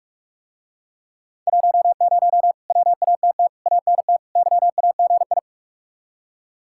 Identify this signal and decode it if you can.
Morse code, heard as a keyed tone, about 45 wpm, 700 Hz. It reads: 10 WATT ANT YAGI